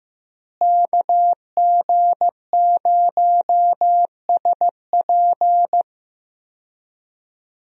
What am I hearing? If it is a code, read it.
Morse code, heard as a keyed tone, about 16 wpm, 700 Hz. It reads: KG0SP
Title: USA Call Signs - 5 Characters - Repeated 3x - 15wpm